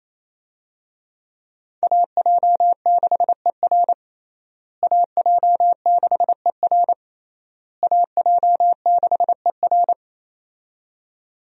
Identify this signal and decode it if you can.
Morse code, heard as a keyed tone, about 28 wpm, 700 Hz. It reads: AJ6ER AJ6ER AJ6ER
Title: USA Call Signs - 5 Characters - Repeated 3x - 28wpm